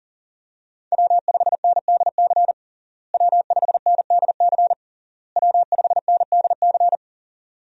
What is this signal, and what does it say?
Morse code, heard as a keyed tone, about 37 wpm, 700 Hz. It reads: W5NDC W5NDC W5NDC
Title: USA Call Signs - 5 Characters - Repeated 3x - 40wpm